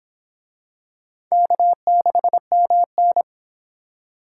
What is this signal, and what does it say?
Morse code, heard as a keyed tone, about 23 wpm, 700 Hz. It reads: K6MN